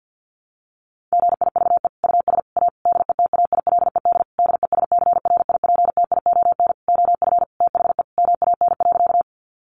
Morse code, heard as a keyed tone, about 50 wpm, 700 Hz. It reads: GIVE US A DETAILED DESCRIPTION OF THE MAN.
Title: Sentences from Top 1500 Words 50wpm